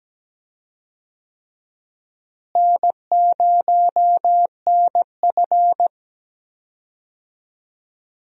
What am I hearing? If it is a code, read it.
Morse code, heard as a keyed tone, about 17 wpm, 700 Hz. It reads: N0NF
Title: USA Call Signs - 4 Characters - Repeated 3x - 17wpm